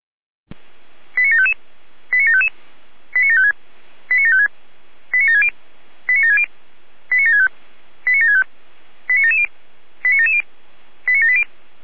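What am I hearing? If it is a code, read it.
Five-tone selective call (ZVEI), heard as quick runs of five tones, tones of about 70 ms. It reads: 89755 89755 89756 89756 89760 89760 89765 89765 89770 89770 89780
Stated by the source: ZVEI five-tone selective call received on an SDR.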